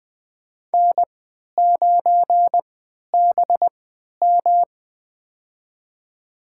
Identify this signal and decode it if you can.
Morse code, heard as a keyed tone, about 21 wpm, spaced as for 12 wpm, 700 Hz. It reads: N9BM
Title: USA Call Signs - 4 Characters - 3x Character Spacing - 20wpm